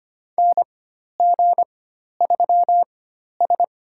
Morse code, heard as a keyed tone, about 25 wpm, spaced as for 12 wpm, 700 Hz. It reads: NG3S
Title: USA Call Signs - 4 Characters - 4x Character Spacing - 25wpm